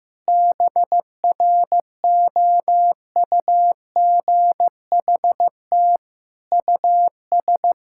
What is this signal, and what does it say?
Morse code, heard as a keyed tone, about 15 wpm, 700 Hz. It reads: BROUGHT US
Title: Sentences from Top 1000 Words - Review Part 6 - 15wpm